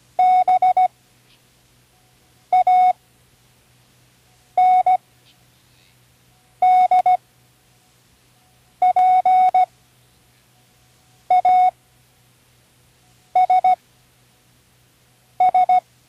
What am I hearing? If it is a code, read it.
Morse code, heard as a keyed tone, about 16 wpm, spaced as for 5 wpm, 740 Hz. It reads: BANDPASS
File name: Learning Morse Code (from Tune in the World)